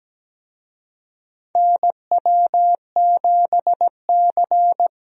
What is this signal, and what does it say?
Morse code, heard as a keyed tone, about 17 wpm, 700 Hz. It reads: NW7C
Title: USA Call Signs - 4 Characters - Repeated 3x - 17wpm